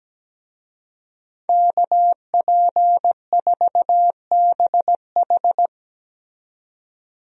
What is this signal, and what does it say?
Morse code, heard as a keyed tone, about 17 wpm, 700 Hz. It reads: KP4BH
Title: DXCC-100 Call Signs - 5 Characters - 17wpm